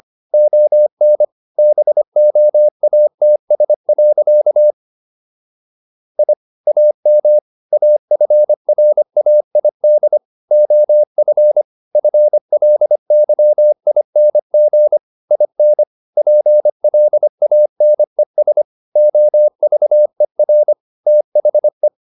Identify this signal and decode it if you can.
Morse code, heard as a keyed tone, about 25 wpm, 600 Hz. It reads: ON BOATS. I AM AFRAID OF FLYING IN PLANES OVER THE